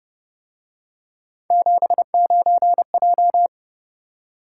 Morse code, heard as a keyed tone, about 29 wpm, 700 Hz. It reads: 79J
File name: Speed-Racing - Three Letter-Number Combinations - 25wpm